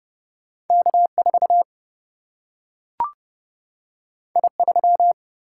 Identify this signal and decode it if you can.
Morse code, heard as a keyed tone, about 30 wpm, 700 Hz. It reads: K 4 I 3
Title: Single-Letter Single-Number 30wpm